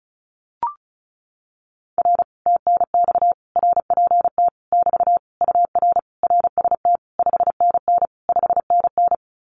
Morse code, heard as a keyed tone, about 36 wpm, 700 Hz. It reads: R TNX RPT <BT> UR RST 5NN 5NN
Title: Common QSO Phrases Encore 35wpm